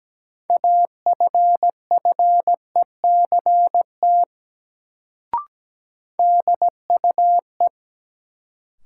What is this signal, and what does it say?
Morse code, heard as a keyed tone, about 17 wpm, 700 Hz. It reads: AFFECT DUE
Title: Sentences from Top 900 Words - New Word Drill - 17wpm